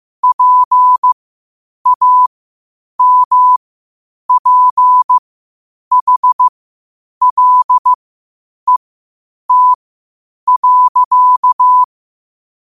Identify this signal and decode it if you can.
Morse code, heard as a keyed tone, about 15 wpm, spaced as for 8 wpm, 1000 Hz. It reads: PAMPHLET.